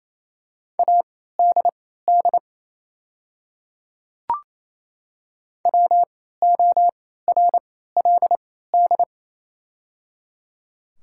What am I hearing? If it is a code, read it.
Morse code, heard as a keyed tone, about 28 wpm, spaced as for 16 wpm, 700 Hz. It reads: ADD WORLD
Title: 3x Character Spacing - Top 200 Words - 28wpm